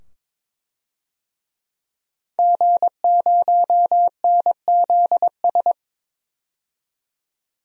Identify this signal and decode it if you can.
Morse code, heard as a keyed tone, about 22 wpm, 700 Hz. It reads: G0NZS
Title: DXCC-100 Call Signs - 5 Characters - 22wpm